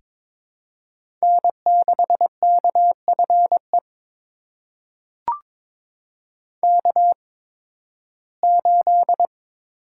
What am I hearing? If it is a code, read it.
Morse code, heard as a keyed tone, about 22 wpm, spaced as for 14 wpm, 700 Hz. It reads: N6KFE K8
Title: USA Call Signs - 5 Characters - 8x Character Spacing - 22wpm